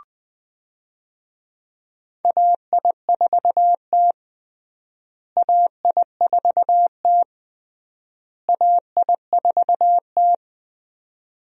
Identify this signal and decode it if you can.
Morse code, heard as a keyed tone, about 20 wpm, 700 Hz. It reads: AI4T AI4T AI4T
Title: USA Call Signs - 4 Characters - Repeated 3x - 20wpm